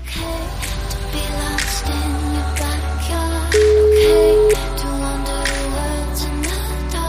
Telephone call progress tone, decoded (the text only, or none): ringback tone